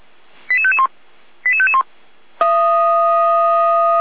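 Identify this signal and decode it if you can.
Five-tone selective call (ZVEI), heard as quick runs of five tones, tones of about 70 ms. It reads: 88551 88551
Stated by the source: ZVEI five-tone call received by SDR, from the Feuerwehr Orlamünde fire brigade server.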